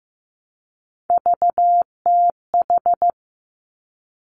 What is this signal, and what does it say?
Morse code, heard as a keyed tone, about 15 wpm, 700 Hz. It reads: VTH
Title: Three Challenging Letter-Number Combinations 15wpm